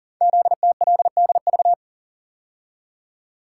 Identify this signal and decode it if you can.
Morse code, heard as a keyed tone, about 38 wpm, 700 Hz. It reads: ZTLDV
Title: Code Groups - 5 Letters - 40wpm